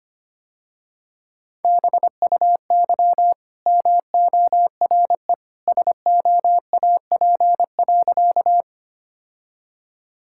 Morse code, heard as a keyed tone, about 25 wpm, 700 Hz. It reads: BUY MORE SOAP.
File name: Easy Sentences with Hints - 25wpm